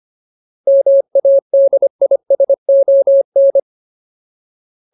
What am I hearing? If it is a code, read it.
Morse code, heard as a keyed tone, about 25 wpm, 550 Hz. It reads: MADISON
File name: Ultimate Word List / 25WPM Ultimate